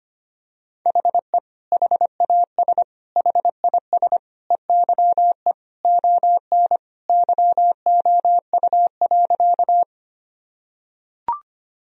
Morse code, heard as a keyed tone, about 25 wpm, 700 Hz. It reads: HE HAS HIS EYE ON YOU.
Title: Sentences from Top 1000 Words - Review Part 1 - 25wpm